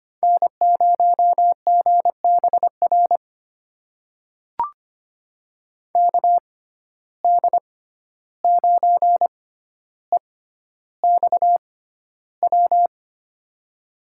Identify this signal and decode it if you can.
Morse code, heard as a keyed tone, about 25 wpm, spaced as for 12 wpm, 700 Hz. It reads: N0GBR KD9EXW
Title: USA Call Signs - 6x Character Spacing - 25wpm